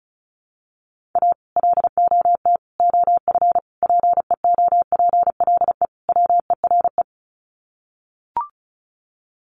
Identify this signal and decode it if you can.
Morse code, heard as a keyed tone, about 34 wpm, 700 Hz. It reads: A LOT OF PEOPLE WERE
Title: Sets of 5 Words Encore2 35wpm